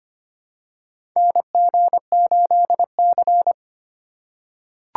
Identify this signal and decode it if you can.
Morse code, heard as a keyed tone, about 25 wpm, 700 Hz. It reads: NG8C
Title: USA Call Signs - 3x Character Spacing - 25wpm